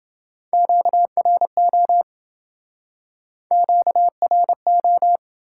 Morse code, heard as a keyed tone, about 29 wpm, 700 Hz. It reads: QRO QRO